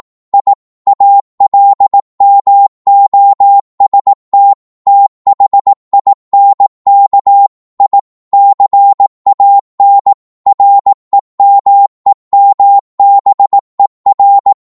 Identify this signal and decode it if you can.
Morse code, heard as a keyed tone, about 18 wpm, 800 Hz. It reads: I ALMOST THINK I CAN REMEMBER
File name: Morse / psg-18wpm